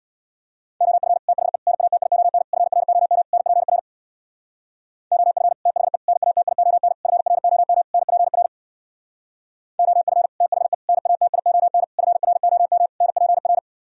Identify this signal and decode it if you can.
Morse code, heard as a keyed tone, about 61 wpm, 700 Hz. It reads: OF THE NATION FROM NPR OF THE NATION FROM NPR OF THE NATION FROM NPR
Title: Speed-Racing - Sets of 5 Words 50wpm